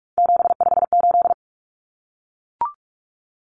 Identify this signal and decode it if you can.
Morse code, heard as a keyed tone, about 44 wpm, 700 Hz. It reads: 758
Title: Numbers - 3-Digit 45wpm